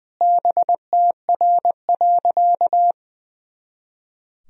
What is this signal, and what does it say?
Morse code, heard as a keyed tone, about 20 wpm, 700 Hz. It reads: BTR.
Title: Sentences from Top 500 Words and Abbreviations 20wpm